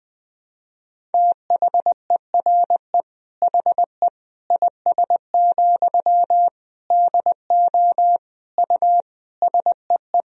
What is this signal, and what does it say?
Morse code, heard as a keyed tone, about 20 wpm, 700 Hz. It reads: THERE HE IS, DO U SEE
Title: Sentences from Top 100 Words and Abbreviations 20wpm